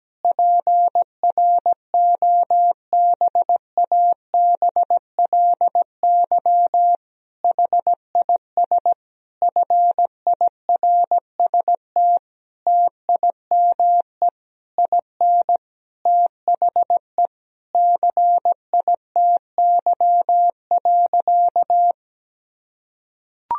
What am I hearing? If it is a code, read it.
Morse code, heard as a keyed tone, about 17 wpm, 700 Hz. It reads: PROBABLY HIS FIRST TIME IN THE CITY.
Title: Sentences from Top 1000 Words - Review Part 4 - 17wpm